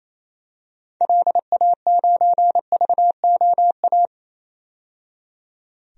Morse code, heard as a keyed tone, about 28 wpm, 700 Hz. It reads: LA9VOA
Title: DXCC-100 Call Signs - 6 Characters - 28wpm